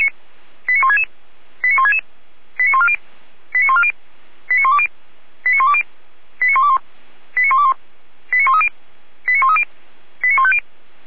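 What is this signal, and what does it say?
Five-tone selective call (ZVEI), heard as quick runs of five tones, tones of about 70 ms. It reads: 89166 89166 89140 89140 89120 89120 89121 89121 89130 89130 89150
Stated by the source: ZVEI five-tone selective call received on SDR.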